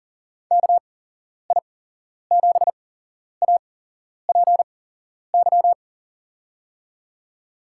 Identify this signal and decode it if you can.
Morse code, heard as a keyed tone, about 40 wpm, spaced as for 11 wpm, 700 Hz. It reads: KI7APY
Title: USA Call Signs - 6 Characters - 8x Character Spacing - 40wpm